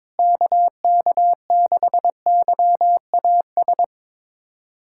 Morse code, heard as a keyed tone, about 22 wpm, 700 Hz. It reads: KK6YAS
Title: USA Callsign DE Callsign - 6 Characters - 22wpm